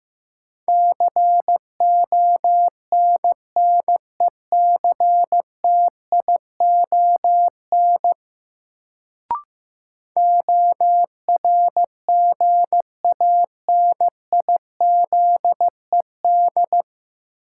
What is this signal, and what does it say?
Morse code, heard as a keyed tone, about 16 wpm, 700 Hz. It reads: CONNECTION ORGANIZED
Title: Sentences from Top 1300 Words - New Word Drill - 15wpm